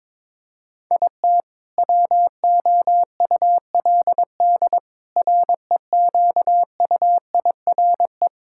Morse code, heard as a keyed tone, about 22 wpm, 700 Hz. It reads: IT WOULD REQUIRE
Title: Sentences from Top 1000 Words - Review Part 6 - 22wpm